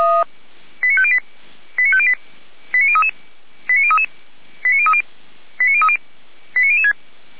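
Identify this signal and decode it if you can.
Five-tone selective call (ZVEI), heard as quick runs of five tones, tones of about 70 ms. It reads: 89508 89508 89033 89033 89030 89030 89006
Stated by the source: Five-tone ZVEI selective call received on an SDR.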